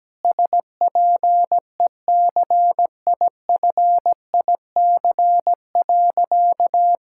Morse code, heard as a keyed tone, about 17 wpm, 700 Hz. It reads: SPECIFIC.
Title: Sentences from Top 1000 Words - Review Part 6 - 17wpm